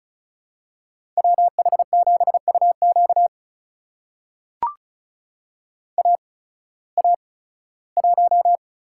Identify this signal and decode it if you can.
Morse code, heard as a keyed tone, about 36 wpm, spaced as for 19 wpm, 700 Hz. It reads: WH7UQ AA1
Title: USA Call Signs - 5 Characters - 8x Character Spacing - 35wpm